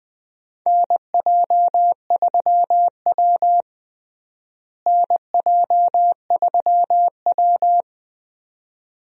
Morse code, heard as a keyed tone, about 21 wpm, 700 Hz. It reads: NJ3W NJ3W
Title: USA Call Signs - 4 Characters - Repeated 2x - 20wpm